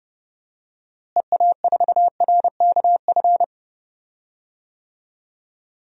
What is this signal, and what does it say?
Morse code, heard as a keyed tone, about 30 wpm, 700 Hz. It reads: EA4RKF
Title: DXCC-100 Call Signs - 6 Characters - Repeated 3x - 30wpm